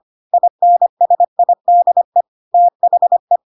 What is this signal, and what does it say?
Morse code, heard as a keyed tone, about 25 wpm, 700 Hz. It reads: INSIDE THE
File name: Pinocchio / Pinocchio ch1 25 25 ik1hge